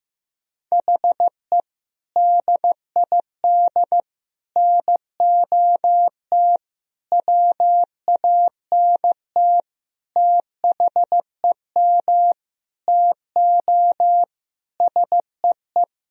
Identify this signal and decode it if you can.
Morse code, heard as a keyed tone, about 15 wpm, 700 Hz. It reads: HE DID NOT WANT THEM TO SEE
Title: Sentences from Top 1000 Words - Review Part 1 - 15wpm